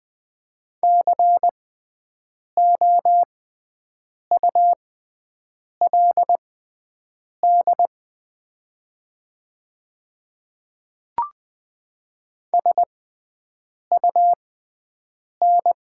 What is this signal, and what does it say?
Morse code, heard as a keyed tone, about 20 wpm, spaced as for 7 wpm, 700 Hz. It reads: COULD SUN
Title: 6x Character Spacing - Top 300 Words - 20wpm